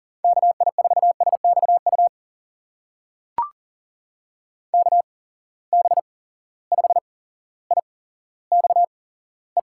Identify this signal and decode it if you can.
Morse code, heard as a keyed tone, about 40 wpm, spaced as for 17 wpm, 700 Hz. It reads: KI4SXU KB5IXE
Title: USA Call Signs - 6 Characters - 8x Character Spacing - 40wpm